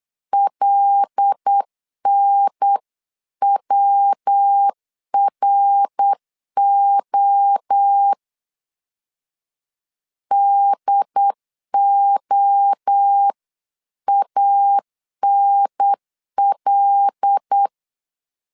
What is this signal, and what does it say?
Morse code, heard as a keyed tone, about 8 wpm, 800 Hz. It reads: LNWRO DOANL
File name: CW / DET / 1 6 1500C S B 8wpm 5142